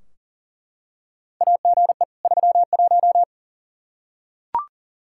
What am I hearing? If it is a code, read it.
Morse code, heard as a keyed tone, about 40 wpm, 700 Hz. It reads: AGE 31
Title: Practice QSO - Exchange 3 - More Info - 40wpm